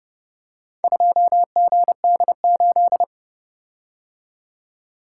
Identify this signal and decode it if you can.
Morse code, heard as a keyed tone, about 30 wpm, 700 Hz. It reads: 2GD8